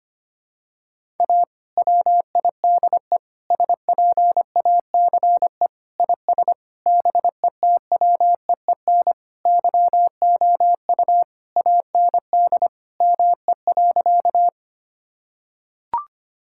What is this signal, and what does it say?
Morse code, heard as a keyed tone, about 25 wpm, 700 Hz. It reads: A WIDE SPACE IS BETWEEN YOU AND ME.
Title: Sentences from Top 1000 Words - Review Part 6 - 25wpm